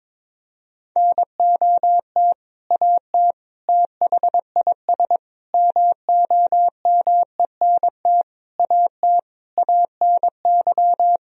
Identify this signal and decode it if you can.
Morse code, heard as a keyed tone, about 22 wpm, 700 Hz. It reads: NOT AT THIS MOMENT AT ANY
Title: Sentences from Top 1000 Words - Review Part 4 - 22wpm